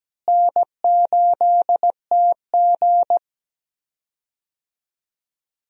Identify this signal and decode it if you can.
Morse code, heard as a keyed tone, about 17 wpm, 700 Hz. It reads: N8TG
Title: USA Call Signs - 4 Characters - Repeated 3x - 17wpm